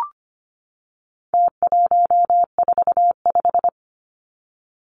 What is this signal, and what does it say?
Morse code, heard as a keyed tone, about 25 wpm, 700 Hz. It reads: T145
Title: Numbers - Contest Style - 4-Digit 25wpm